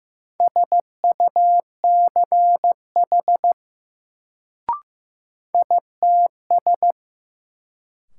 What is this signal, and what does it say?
Morse code, heard as a keyed tone, about 15 wpm, 700 Hz. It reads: SUCH ITS
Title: Sets of 2 Words - New Word Drill - 15wpm